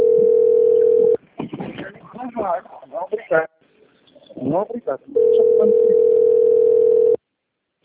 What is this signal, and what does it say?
Ringback tone (North America), 440 + 480 Hz, with speech.
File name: rg-1600-unknown- - - .10109